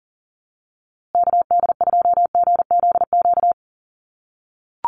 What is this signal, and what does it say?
Morse code, heard as a keyed tone, about 40 wpm, 700 Hz. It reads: KD2GZQ
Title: USA Call Signs Repeated 3x 40wpm